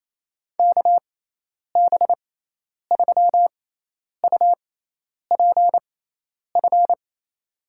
Morse code, heard as a keyed tone, about 27 wpm, spaced as for 10 wpm, 700 Hz. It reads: KB3UPF
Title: USA Call Signs - 6 Characters - 6x Character Spacing - 28wpm